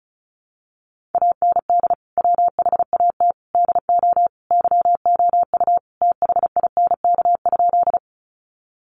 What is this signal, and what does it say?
Morse code, heard as a keyed tone, about 34 wpm, 700 Hz. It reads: AND WHAT DO YOU THINK?
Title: Sentences from Top 200 Words 35wpm